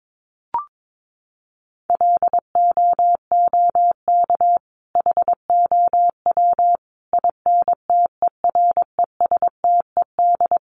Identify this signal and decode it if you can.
Morse code, heard as a keyed tone, about 22 wpm, 700 Hz. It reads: LOOK HOW INTERESTED
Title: Sentences from Top 1400 Words 22wpm